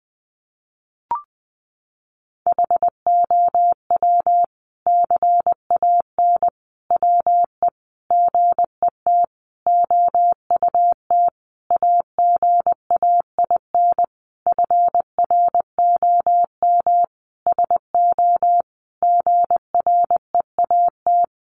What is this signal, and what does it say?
Morse code, heard as a keyed tone, about 20 wpm, 700 Hz. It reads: HOW CAN WE GET OUT AGAIN FROM SO GREAT
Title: Sentences from Top 1200 Words 20wpm